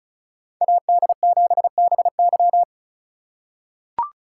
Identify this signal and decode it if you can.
Morse code, heard as a keyed tone, about 36 wpm, 700 Hz. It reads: AD7BY
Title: USA Call Signs - 3x Character Spacing - 35wpm